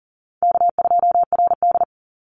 Morse code, heard as a keyed tone, about 40 wpm, 700 Hz. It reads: K2RD
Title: Call Signs Repeated 3x - 40wpm